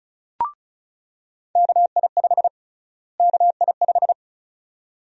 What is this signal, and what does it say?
Morse code, heard as a keyed tone, about 36 wpm, 700 Hz. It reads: KI5 KI5
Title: USA Call Sign Prefixes - Repeated 2x - 35wpm